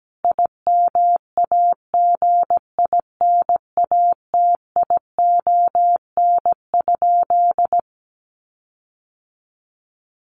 Morse code, heard as a keyed tone, about 17 wpm, 700 Hz. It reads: IMAGINATION?
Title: Sentences from Top 1300 Words 17wpm